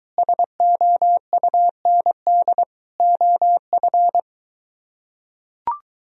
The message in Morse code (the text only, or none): SOUND OF